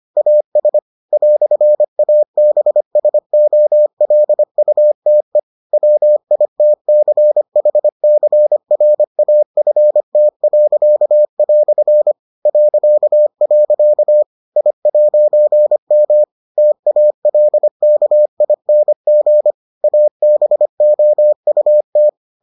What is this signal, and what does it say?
Morse code, heard as a keyed tone, about 25 wpm, 600 Hz. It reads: AS "ABSOLUTE WITCHCRAFT." .. I'M TALKING ABOUT